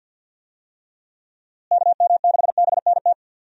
Morse code, heard as a keyed tone, about 52 wpm, 700 Hz. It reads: KM6BNT
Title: USA Call Signs - 6 Characters - Repeated 2x - 50wpm